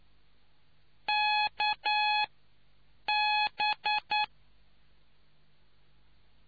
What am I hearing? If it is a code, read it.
Morse code, heard as a keyed tone, about 9 wpm, spaced as for 6 wpm, 840 Hz. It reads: KB